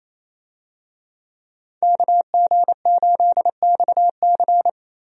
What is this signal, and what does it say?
Morse code, heard as a keyed tone, about 28 wpm, 700 Hz. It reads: KG8XC
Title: USA Call Signs - 5 Characters - Repeated 3x - 28wpm